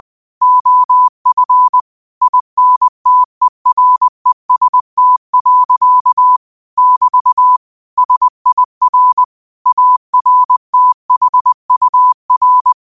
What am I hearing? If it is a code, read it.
Morse code, heard as a keyed tone, about 20 wpm, 1000 Hz. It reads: OF INTEREST. <BT> SIR ARTHUR
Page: New quotes every day in morse code at 20 Words per minute.